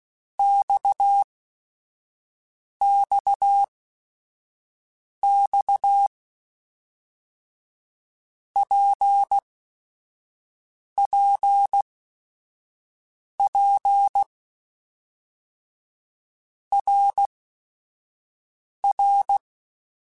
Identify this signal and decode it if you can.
Morse code, heard as a keyed tone, about 16 wpm, spaced as for 5 wpm, 800 Hz. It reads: XXX PPP RR
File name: Basic Sets / Basic Letter Groups Collective7 / RKPX@5